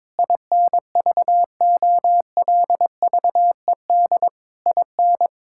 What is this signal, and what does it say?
Morse code, heard as a keyed tone, about 22 wpm, 700 Hz. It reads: INVOLVED IN
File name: practice / 22wpm / Sets of 6 Words - 22wpm